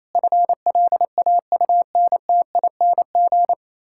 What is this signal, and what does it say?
Morse code, heard as a keyed tone, about 27 wpm, 700 Hz. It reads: FLAUNTING
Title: Pangram Challenge v2 - 28wpm